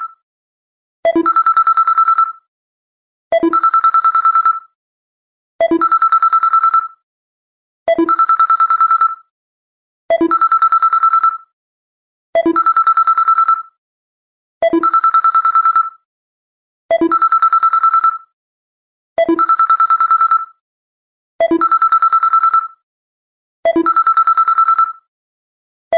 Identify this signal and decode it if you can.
There is no code.